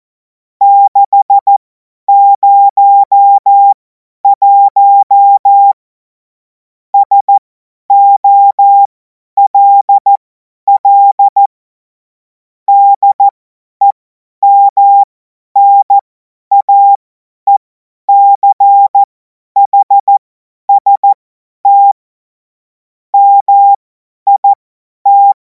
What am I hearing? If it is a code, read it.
Morse code, heard as a keyed tone, about 14 wpm, spaced as for 10 wpm, 800 Hz. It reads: 601 SOLL DEMNAECHST MIT
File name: Morsekurs / Witze / Witz 23. Juni 2026 50 F70